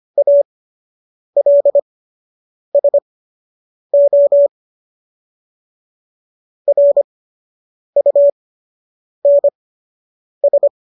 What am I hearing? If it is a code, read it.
Morse code, heard as a keyed tone, about 25 wpm, spaced as for 8 wpm, 570 Hz. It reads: ALSO RUNS
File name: qso006 08